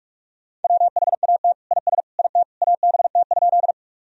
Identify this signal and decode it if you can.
Morse code, heard as a keyed tone, about 47 wpm, 700 Hz. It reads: WHAT IS IT ABT?
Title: Sentences from Top 100 Words and Abbreviations 45wpm